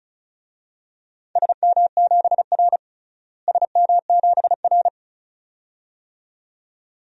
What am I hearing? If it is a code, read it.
Morse code, heard as a keyed tone, about 36 wpm, 700 Hz. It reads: SM7R SM7R
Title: DXCC-100 Call Signs - 4 Characters - Repeated 2x - 35wpm